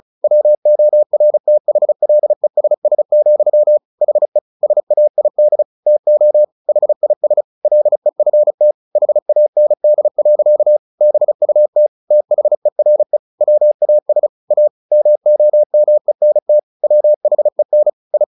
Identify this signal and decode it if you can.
Morse code, heard as a keyed tone, about 34 wpm, 600 Hz. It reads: WORTHLESS, HE SAID TO HIS LEFT HAND. BUT THERE WAS A MOMENT WHEN I
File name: the old man and the sea 2 35wpm